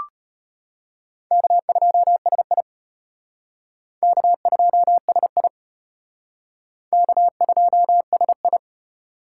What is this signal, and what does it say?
Morse code, heard as a keyed tone, about 31 wpm, 700 Hz. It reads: K2SI K2SI K2SI
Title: Speed Racing - USA 1x2 Call Signs 25wpm